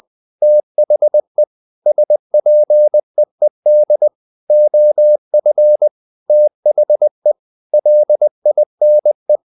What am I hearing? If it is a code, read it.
Morse code, heard as a keyed tone, about 20 wpm, 600 Hz. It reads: THE SPEED OF THE LINE
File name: the old man and the sea part 2 20wpm